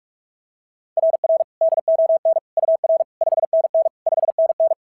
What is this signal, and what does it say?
Morse code, heard as a keyed tone, about 44 wpm, 660 Hz. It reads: RR DON UR 5NN 5NN
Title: SOTA QSO Practice 45wpm